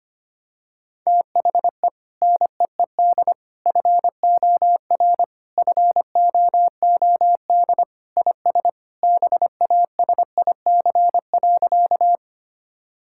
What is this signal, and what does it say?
Morse code, heard as a keyed tone, about 25 wpm, 700 Hz. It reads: THE NEED FOR FOOD IS BASIC.
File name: practice / 25wpm / Sentences with 3-Word Phrases - 25wpm